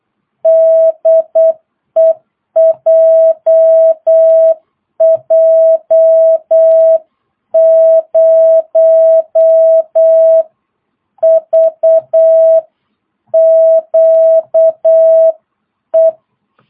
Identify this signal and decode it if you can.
Morse code, heard as a keyed tone, about 8 wpm, 650 Hz. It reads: DE JJ0VQE
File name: jh8cbh / 2022cw